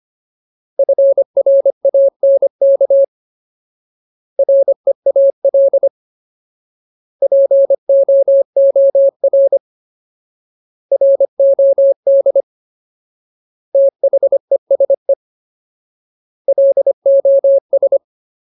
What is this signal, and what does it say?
Morse code, heard as a keyed tone, about 25 wpm, 550 Hz. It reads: FRANK REAL POOR ROD THESE LOS
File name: Ultimate Word List / 25WPM Ultimate